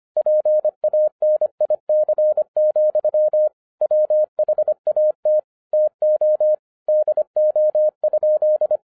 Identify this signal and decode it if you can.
Morse code, heard as a keyed tone, about 25 wpm, 610 Hz. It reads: PANIC, WHAT TO DO?
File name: FIRST FISTS CONTACT-25wpm